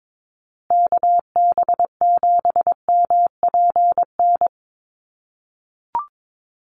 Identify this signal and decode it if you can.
Morse code, heard as a keyed tone, about 22 wpm, 700 Hz. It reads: KB7MPN
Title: USA Call Signs Repeated 3x 22wpm